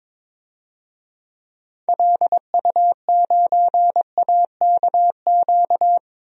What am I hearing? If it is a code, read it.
Morse code, heard as a keyed tone, about 22 wpm, 700 Hz. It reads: LU9AKQ
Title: DXCC-100 Call Signs - 6 Characters - Repeated 3x - 22wpm